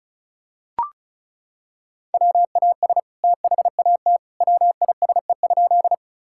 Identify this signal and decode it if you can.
Morse code, heard as a keyed tone, about 34 wpm, 700 Hz. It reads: WAS THAT WISE?
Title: Sentences from Top 1600 Words 35wpm